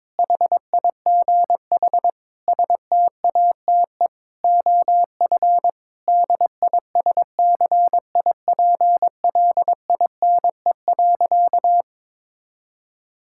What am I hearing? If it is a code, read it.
Morse code, heard as a keyed tone, about 21 wpm, 700 Hz. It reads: HIGH STATE OF DISCIPLINE.
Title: Sentences from Top 1000 Words - Review Part 6 - 22wpm